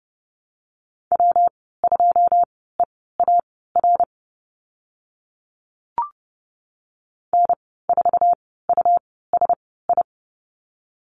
Morse code, heard as a keyed tone, about 30 wpm, spaced as for 17 wpm, 700 Hz. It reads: W2EAR N4USI
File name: Lesson 19g - Letter w - Call Signs 30wpm